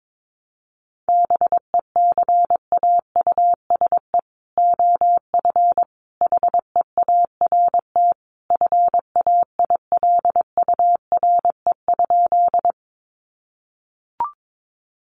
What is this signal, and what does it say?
Morse code, heard as a keyed tone, about 22 wpm, 700 Hz. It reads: BECAUSE OF HEART FAILURE?
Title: Sentences from Top 1200 Words 22wpm